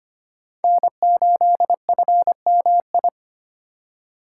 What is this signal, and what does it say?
Morse code, heard as a keyed tone, about 25 wpm, 700 Hz. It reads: N8FMI
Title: USA Call Signs - 5 Characters - 2x Character Spacing - 25wpm